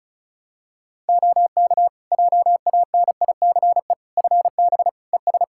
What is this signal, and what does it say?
Morse code, heard as a keyed tone, about 36 wpm, 700 Hz. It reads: OK JANICE FB ES
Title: K1USN SST QSO Practice 35wpm